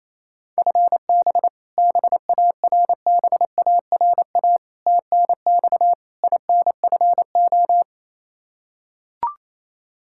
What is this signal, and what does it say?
Morse code, heard as a keyed tone, about 28 wpm, 700 Hz. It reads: FB BARBARA TNX INFO
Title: Practice QSO - Exchange 3 - More Info - 28wpm